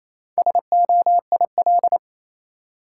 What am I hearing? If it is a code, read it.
Morse code, heard as a keyed tone, about 28 wpm, 700 Hz. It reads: SOIL
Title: Sentences from Top 1000 Words - Review Part 4 - New Word Drill - 28wpm